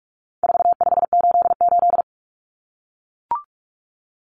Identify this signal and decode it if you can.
Morse code, heard as a keyed tone, about 44 wpm, 700 Hz. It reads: 4588
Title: Common Cut Numbers - 4-Digit 45wpm